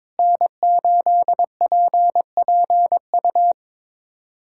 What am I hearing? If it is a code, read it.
Morse code, heard as a keyed tone, about 23 wpm, 700 Hz. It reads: N8PPU
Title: USA Callsign DE Callsign - 5 Characters - 22wpm